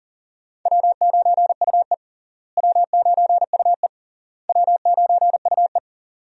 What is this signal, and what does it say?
Morse code, heard as a keyed tone, about 40 wpm, 700 Hz. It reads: W9UE W9UE W9UE
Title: USA Call Signs - 4 Characters - Repeated 3x - 40wpm